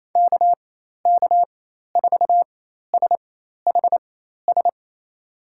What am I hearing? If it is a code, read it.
Morse code, heard as a keyed tone, about 28 wpm, spaced as for 13 wpm, 700 Hz. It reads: KK4SHS
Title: USA Call Signs - 6 Characters - 4x Character Spacing - 28wpm